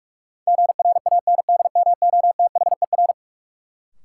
Morse code, heard as a keyed tone, about 41 wpm, 700 Hz. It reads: GRANDMOTHER
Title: Sentences from Top 1200 Words - New Word Drill - 45wpm